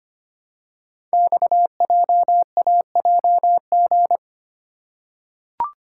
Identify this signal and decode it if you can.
Morse code, heard as a keyed tone, about 25 wpm, 700 Hz. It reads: XJAJG
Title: Code Groups - 5 Letters - 25wpm